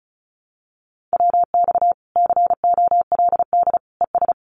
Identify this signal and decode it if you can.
Morse code, heard as a keyed tone, about 34 wpm, 700 Hz. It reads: WX COLD ES
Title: Common QSO Phrases 35wpm